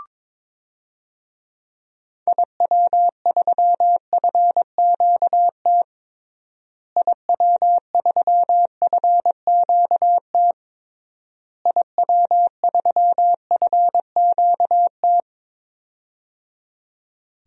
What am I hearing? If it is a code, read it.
Morse code, heard as a keyed tone, about 22 wpm, 700 Hz. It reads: IW3FQT IW3FQT IW3FQT
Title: DXCC-100 Call Signs - 6 Characters - Repeated 3x - 22wpm